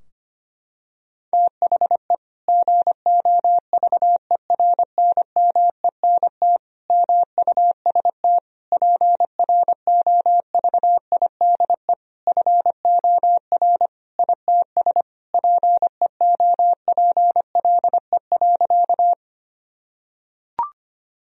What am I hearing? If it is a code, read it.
Morse code, heard as a keyed tone, about 25 wpm, 700 Hz. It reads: THE GOVERNMENT MUST PROVIDE FOR ITS PEOPLE.
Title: Sentences from Top 1000 Words - Review Part 3 - 25wpm